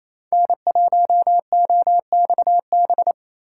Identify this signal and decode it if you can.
Morse code, heard as a keyed tone, about 28 wpm, 700 Hz. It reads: N1OXB
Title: USA Call Signs - 5 Characters - 2x Character Spacing - 28wpm